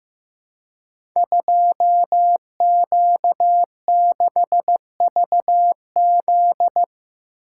Morse code, heard as a keyed tone, about 15 wpm, 700 Hz. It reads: 2Q6VZ
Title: Code Groups - 5 Letter:Numbers - 15wpm